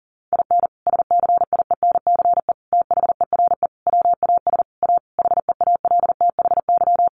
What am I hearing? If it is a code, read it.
Morse code, heard as a keyed tone, about 40 wpm, 700 Hz. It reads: IN SCIENCE THERE WAS A HEALTHY
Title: Sentences from Top 1400 Words 40wpm